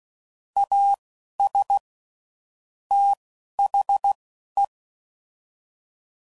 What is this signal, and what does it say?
Morse code, heard as a keyed tone, about 16 wpm, spaced as for 11 wpm, 800 Hz. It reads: AS THE
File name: CW Code Groups @10WPM / Technical Subjects / AND Gate